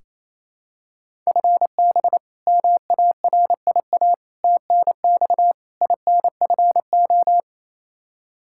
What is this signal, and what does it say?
Morse code, heard as a keyed tone, about 27 wpm, 700 Hz. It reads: FB MARIA TNX INFO
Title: K1USN SST QSO Practice 28wpm